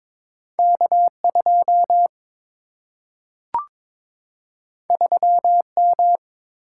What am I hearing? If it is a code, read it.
Morse code, heard as a keyed tone, about 22 wpm, 700 Hz. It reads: K 2 3 M
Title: Single-Letter Single-Number 22wpm